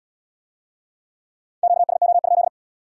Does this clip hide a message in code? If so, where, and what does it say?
Yes: Morse code from 1.6 to 2.5 s, heard as a keyed tone, about 66 wpm, 700 Hz. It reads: QSO?